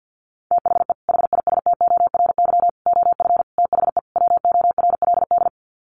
Morse code, heard as a keyed tone, about 50 wpm, 700 Hz. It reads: THE HISTORY OF THE WORLD
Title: Sets of 5 Words Encore2 50wpm